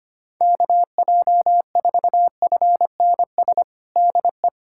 Morse code, heard as a keyed tone, about 25 wpm, 700 Hz. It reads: KJ4FNS DE
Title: USA Callsign DE Callsign 25wpm